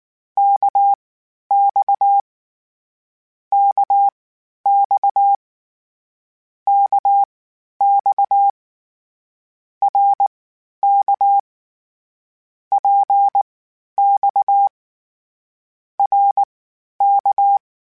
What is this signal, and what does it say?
Morse code, heard as a keyed tone, about 19 wpm, spaced as for 11 wpm, 800 Hz. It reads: KX KX KX RK PX RK
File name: Short Runs / Short Runs-mp3s / Balanced @18